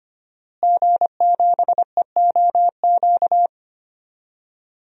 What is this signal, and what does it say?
Morse code, heard as a keyed tone, about 25 wpm, 700 Hz. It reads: G7EOQ
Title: DXCC-100 Call Signs - 5 Characters - Repeated 2x - 25wpm